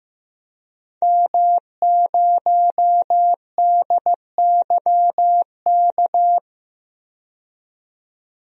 Morse code, heard as a keyed tone, about 15 wpm, 700 Hz. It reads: M0DYK
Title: DXCC-100 Call Signs - 5 Characters - 15wpm